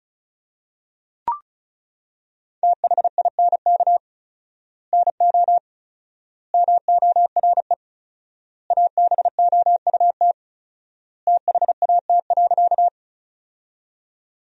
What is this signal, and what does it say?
Morse code, heard as a keyed tone, about 36 wpm, 700 Hz. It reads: THINK NO MORE ABOUT THAT.
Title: 4x Word-Spacing - Sentences from Top 200 Words - 35wpm